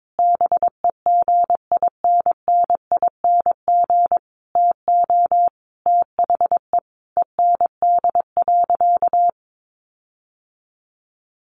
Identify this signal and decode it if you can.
Morse code, heard as a keyed tone, about 22 wpm, 700 Hz. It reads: BEGINNING TO THE END.
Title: Sentences from Top 1300 Words 22wpm